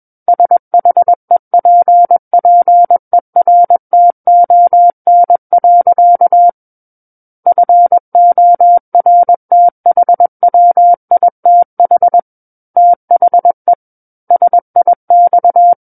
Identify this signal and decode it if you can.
Morse code, heard as a keyed tone, about 21 wpm, 700 Hz. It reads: SHEPPERTON. FORTHWITH THE SIX
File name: Book One / 12-Chapter 12 - 21 WPM